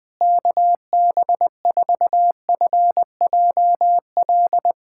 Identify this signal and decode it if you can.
Morse code, heard as a keyed tone, about 20 wpm, 700 Hz. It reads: KB4FJL
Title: USA Callsign DE Callsign - 6 Characters - 20wpm